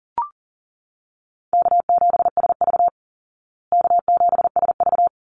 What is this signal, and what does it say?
Morse code, heard as a keyed tone, about 40 wpm, 700 Hz. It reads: K7SV K7SV
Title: Easy Call Signs Repeated 2x - 40wpm